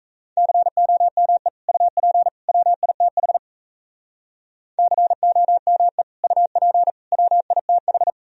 Morse code, heard as a keyed tone, about 34 wpm, 700 Hz. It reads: COME UP WITH COME UP WITH
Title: Speed-Racing - Sets of 3 Words 28wpm